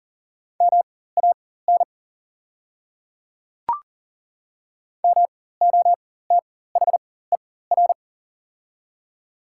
Morse code, heard as a keyed tone, about 39 wpm, spaced as for 18 wpm, 700 Hz. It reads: MAN MOTHER
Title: 4x Character Spacing - Top 200 Words - 40wpm